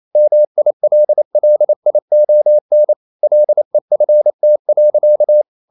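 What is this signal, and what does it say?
Morse code, heard as a keyed tone, about 28 wpm, 600 Hz. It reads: MILLION LEFT.
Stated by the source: Morse code practice at 28 wpm, text of Ken Follett's novel Never in English, from CW med Gnister.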